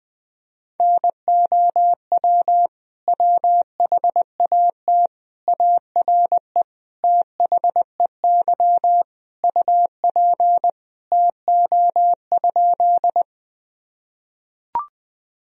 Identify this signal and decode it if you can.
Morse code, heard as a keyed tone, about 20 wpm, 700 Hz. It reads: NOW WHAT ARE THEY UP TO?
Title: Sentences from Top 1000 Words - Review Part 1 - 20wpm